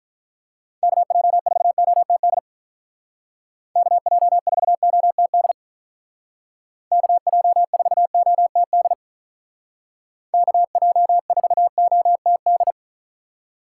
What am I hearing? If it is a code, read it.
Morse code, heard as a keyed tone, about 43 wpm, 700 Hz. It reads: KJ4OTD KJ4OTD KJ4OTD KJ4OTD
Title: Speed Racing - USA Call Signs 35wpm